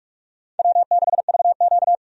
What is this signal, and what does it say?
Morse code, heard as a keyed tone, about 42 wpm, 700 Hz. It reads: W6VQ
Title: USA Call Signs - 4 Characters - 45wpm